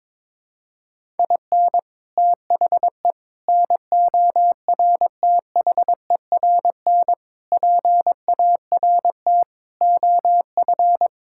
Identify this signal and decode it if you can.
Morse code, heard as a keyed tone, about 22 wpm, 700 Hz. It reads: IN THE NORTHERN PART OF
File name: practice / 22wpm / Sets of 6 Words - 22wpm